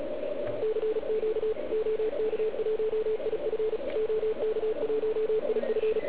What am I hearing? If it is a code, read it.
Morse code, heard as a keyed tone, about 36 wpm, 440 Hz, with radio noise. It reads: CQ OK1IF OK1IF